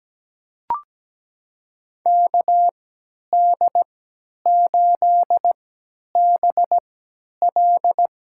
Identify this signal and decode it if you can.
Morse code, heard as a keyed tone, about 17 wpm, spaced as for 10 wpm, 700 Hz. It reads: KD8BL
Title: USA Call Signs - 3x Character Spacing - 17wpm